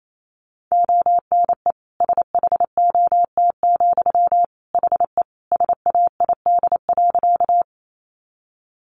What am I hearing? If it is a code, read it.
Morse code, heard as a keyed tone, about 28 wpm, 700 Hz. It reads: ONE SHOT, HE SAID.
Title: Sentences from Top 1100 Words 28wpm